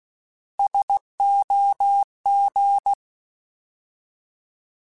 Morse code, heard as a keyed tone, about 16 wpm, 800 Hz. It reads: SOG
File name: Alphabetical / Alphabetical mp3s / Ss @15